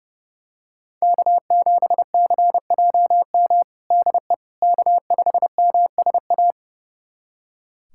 Morse code, heard as a keyed tone, about 32 wpm, 700 Hz. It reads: K7CJM DE K5MSA
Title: USA Callsign DE Callsign - 5 Characters - 30wpm